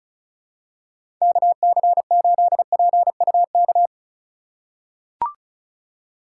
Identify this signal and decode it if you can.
Morse code, heard as a keyed tone, about 36 wpm, 700 Hz. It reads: KC8PUK
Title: USA Call Signs - 6 Characters - 4x Character Spacing - 35wpm